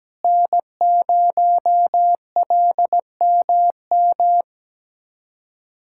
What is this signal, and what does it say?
Morse code, heard as a keyed tone, about 17 wpm, 700 Hz. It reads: N0LMM
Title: USA Callsign DE Callsign 17wpm